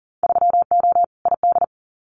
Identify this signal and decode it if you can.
Morse code, heard as a keyed tone, about 40 wpm, 700 Hz. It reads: 3O ID
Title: Winter Field Day Exchanges 40wpm